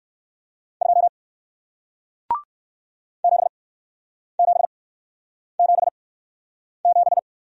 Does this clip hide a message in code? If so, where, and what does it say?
Yes: Morse code from 0.8 to 7.2 s, heard as a keyed tone, about 61 wpm, spaced as for 9 wpm, 700 Hz. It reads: 2 7777